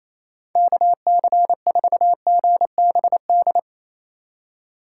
Morse code, heard as a keyed tone, about 28 wpm, 700 Hz. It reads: KC4GBD
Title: USA Call Signs - 6 Characters - 4x Character Spacing - 28wpm